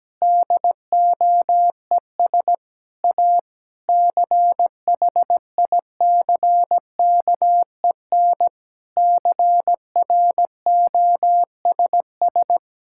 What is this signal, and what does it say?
Morse code, heard as a keyed tone, about 17 wpm, 700 Hz. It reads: DOES A CHICKEN CROSS
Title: Sentences from Top 1000 Words - Review Part 5 - 17wpm